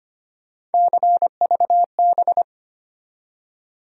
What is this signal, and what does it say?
Morse code, heard as a keyed tone, about 25 wpm, 700 Hz. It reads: CVB
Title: USA Call Sign Postfixes - 3 Letters 25wpm